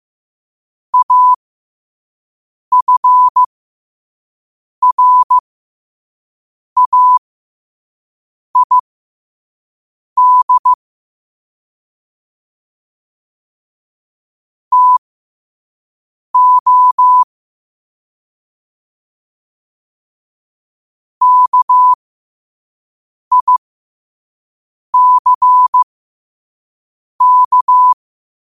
Morse code, heard as a keyed tone, about 15 wpm, spaced as for 5 wpm, 1000 Hz. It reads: AFRAID TO KICK
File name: qotdfiles / 5wpm